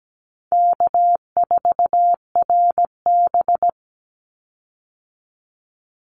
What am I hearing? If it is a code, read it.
Morse code, heard as a keyed tone, about 17 wpm, 700 Hz. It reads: K4RB
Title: Easy Call Signs Repeated 3x - 17wpm